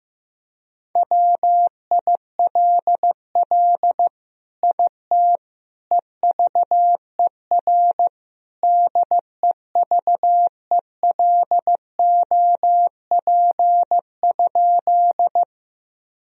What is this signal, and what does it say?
Morse code, heard as a keyed tone, about 15 wpm, 700 Hz. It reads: WILL IT EVER DEVELOP?
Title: Sentences from Top 1000 Words - Review Part 2 - 15wpm